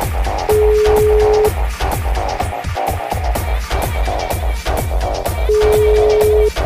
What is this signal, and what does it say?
Ringback tone (Europe, 425 Hz), with noise.